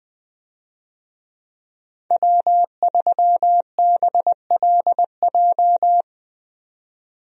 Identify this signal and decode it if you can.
Morse code, heard as a keyed tone, about 21 wpm, 700 Hz. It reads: W3BLJ
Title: USA Call Signs - 5 Characters - 20wpm